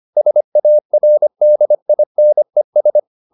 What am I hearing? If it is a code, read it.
Morse code, heard as a keyed tone, about 25 wpm, 600 Hz. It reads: SARDINES
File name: the old man and the sea part 1 25wpm